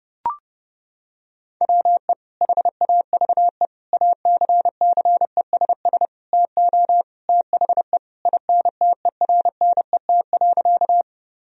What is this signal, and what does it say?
Morse code, heard as a keyed tone, about 30 wpm, 700 Hz. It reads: WE HAVE ACCESS TO THE INTERNET.
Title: Sentences from Top 1000 Words - Review Part 4 - 30wpm